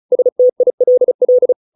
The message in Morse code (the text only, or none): STILL